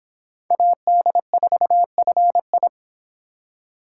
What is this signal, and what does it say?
Morse code, heard as a keyed tone, about 24 wpm, 700 Hz. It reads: AD4FI